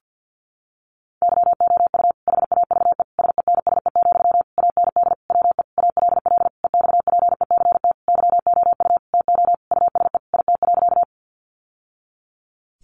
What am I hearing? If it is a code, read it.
Morse code, heard as a keyed tone, about 50 wpm, 700 Hz. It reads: YOU HAVE SENSE, AND WE ALL EXPECT YOU TO USE IT.
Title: Sentences from Top 1100 Words 50wpm